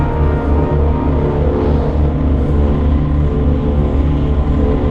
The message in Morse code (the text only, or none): none